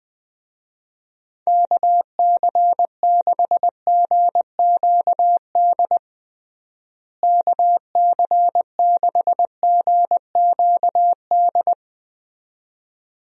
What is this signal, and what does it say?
Morse code, heard as a keyed tone, about 21 wpm, 700 Hz. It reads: KC6GQD KC6GQD
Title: USA Call Signs - 6 Characters - Repeated 2x - 20wpm